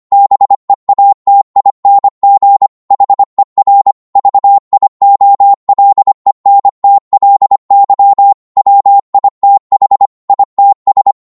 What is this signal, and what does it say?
Morse code, heard as a keyed tone, about 25 wpm, 800 Hz. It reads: BEATING HER VIOLENTLY WITH ITS
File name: Morse / psg-25wpm